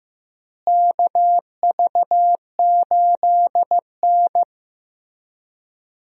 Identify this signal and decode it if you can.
Morse code, heard as a keyed tone, about 15 wpm, 700 Hz. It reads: KV8N
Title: USA Call Signs - 4 Characters - Repeated 3x - 15wpm